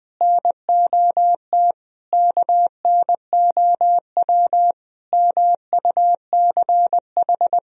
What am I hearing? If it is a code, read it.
Morse code, heard as a keyed tone, about 20 wpm, 700 Hz. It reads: NOT KNOW MUCH
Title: Sentences from Top 1000 Words - Review Part 2 - 20wpm